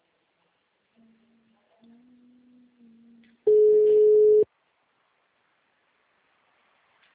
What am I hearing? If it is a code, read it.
Ringback tone (Europe, 425 Hz).